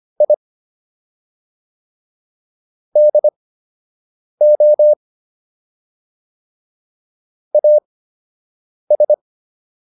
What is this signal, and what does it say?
Morse code, heard as a keyed tone, about 25 wpm, spaced as for 7 wpm, 600 Hz. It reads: I DO AS